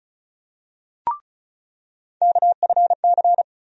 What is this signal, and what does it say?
Morse code, heard as a keyed tone, about 36 wpm, 700 Hz. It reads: KFC
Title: USA Call Sign Postfixes - 3 Letters 35wpm